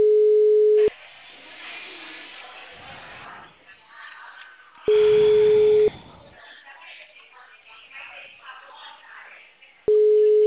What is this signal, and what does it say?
Ringback tone (Europe, 425 Hz).